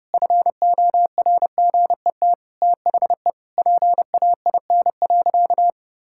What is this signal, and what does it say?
Morse code, heard as a keyed tone, about 30 wpm, 700 Hz. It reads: FORGET THE PAIN.
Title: Sentences from Top 1600 Words 30wpm